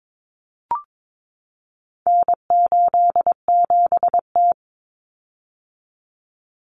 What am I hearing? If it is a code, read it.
Morse code, heard as a keyed tone, about 22 wpm, 700 Hz. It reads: N87T
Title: Common Cut Numbers - 4-Digit 22wpm